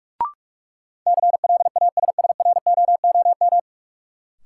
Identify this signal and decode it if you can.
Morse code, heard as a keyed tone, about 46 wpm, 700 Hz. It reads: CLASSROOM
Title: Sentences from Top 1500 Words - New Word Drill Alt - 45wpm